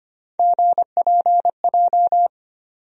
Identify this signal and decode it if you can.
Morse code, heard as a keyed tone, about 25 wpm, 700 Hz. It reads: GPJ
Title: USA Call Sign Postfixes - 3 Letters 25wpm